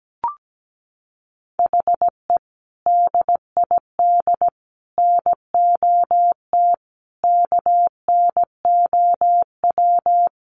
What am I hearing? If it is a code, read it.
Morse code, heard as a keyed tone, about 17 wpm, 700 Hz. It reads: HE DID NOT KNOW
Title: Sets of 4 Words 17wpm